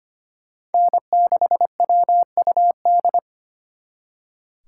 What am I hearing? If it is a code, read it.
Morse code, heard as a keyed tone, about 32 wpm, 700 Hz. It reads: N6WUD